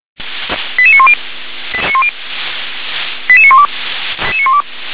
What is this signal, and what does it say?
Five-tone selective call (ZVEI), heard as quick runs of five tones, tones of about 70 ms. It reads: 88910 88912